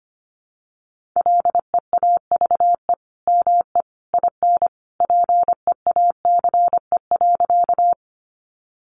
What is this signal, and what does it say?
Morse code, heard as a keyed tone, about 25 wpm, 700 Hz. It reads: LEAVE ME IN PEACE.
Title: Sentences from Top 1500 Words 25wpm